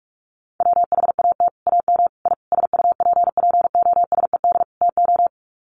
Morse code, heard as a keyed tone, about 44 wpm, 700 Hz. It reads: WHAT AM I SUPPOSED TO
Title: Sets of 5 Words Encore2 45wpm